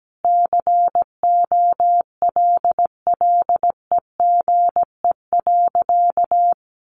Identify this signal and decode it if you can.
Morse code, heard as a keyed tone, about 17 wpm, 700 Hz. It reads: COLLEGE.
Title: Sentences from Top 1300 Words 17wpm